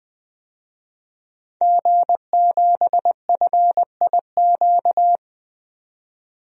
Morse code, heard as a keyed tone, about 20 wpm, 700 Hz. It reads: G7FIQ
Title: DXCC-100 Call Signs - 5 Characters - 20wpm